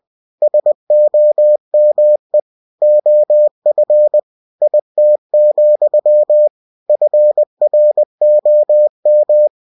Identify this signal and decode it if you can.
Morse code, heard as a keyed tone, about 20 wpm, 600 Hz. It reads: SOME OF IT, FROM
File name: cw-books / winnie-the-pooh / winnie-20wpm-page-0029